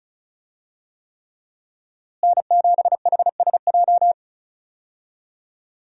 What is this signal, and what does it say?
Morse code, heard as a keyed tone, about 36 wpm, 700 Hz. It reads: N7HSJ
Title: USA Call Signs - 5 Characters - Repeated 3x - 35wpm